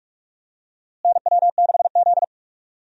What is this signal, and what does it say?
Morse code, heard as a keyed tone, about 41 wpm, 700 Hz. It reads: NW6Z
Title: USA Call Signs - 4 Characters - Repeated 3x - 45wpm